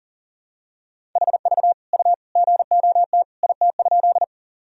Morse code, heard as a keyed tone, about 40 wpm, 700 Hz. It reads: HV U GOT IT?
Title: Sentences from Top 300 Words and Abbreviations 40wpm